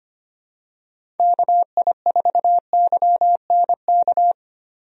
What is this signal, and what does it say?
Morse code, heard as a keyed tone, about 25 wpm, 700 Hz. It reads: KI4YNK
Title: USA Call Signs - 6 Characters - Repeated 3x - 25wpm